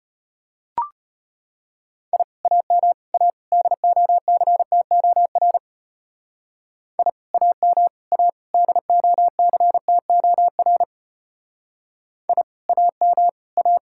Morse code, heard as a keyed tone, about 29 wpm, 700 Hz. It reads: I AM A DOCTOR I AM A DOCTOR I AM A
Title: Speed-Racing Common QSO Phrases Encore - 25wpm